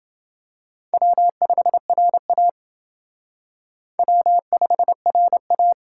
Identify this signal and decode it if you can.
Morse code, heard as a keyed tone, about 28 wpm, 700 Hz. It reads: W5RA W5RA